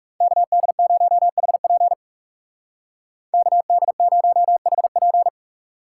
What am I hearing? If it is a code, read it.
Morse code, heard as a keyed tone, about 37 wpm, 700 Hz. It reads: KD0HP KD0HP
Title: Speed Racing - USA Call Signs 30wpm